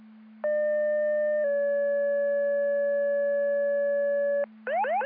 Two-tone page, heard as a one-second tone, then a three-second tone, about 600 then 570 Hz. It is heard from 0.4 to 4.4 s.